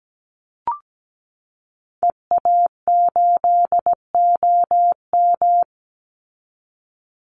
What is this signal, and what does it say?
Morse code, heard as a keyed tone, about 17 wpm, 700 Hz. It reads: EA8OM
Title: Call Signs Repeated 2x - 17wpm